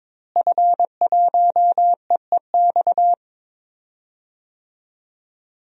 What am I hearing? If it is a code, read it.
Morse code, heard as a keyed tone, about 22 wpm, 700 Hz. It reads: F1EEX
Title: DXCC-100 Call Signs - 5 Characters - Repeated 2x - 22wpm